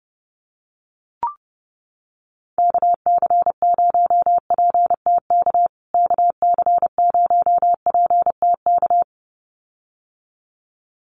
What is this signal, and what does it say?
Morse code, heard as a keyed tone, about 30 wpm, 700 Hz. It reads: KC0PTK KC0PTK
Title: USA Call Signs Repeated 2x 30wpm